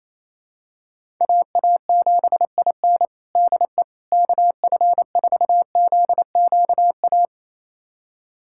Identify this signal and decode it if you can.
Morse code, heard as a keyed tone, about 28 wpm, 700 Hz. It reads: AA7IN DE KF4ZQA
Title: USA Callsign DE Callsign 28wpm